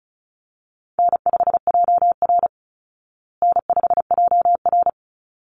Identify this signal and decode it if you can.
Morse code, heard as a keyed tone, about 34 wpm, 700 Hz. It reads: N5JR N5JR
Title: Easy Call Signs Repeated 2x - 35wpm